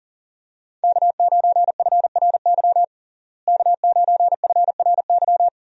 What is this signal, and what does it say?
Morse code, heard as a keyed tone, about 40 wpm, 700 Hz. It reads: K9FRY K9FRY
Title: USA Call Signs - 5 Characters - Repeated 3x - 40wpm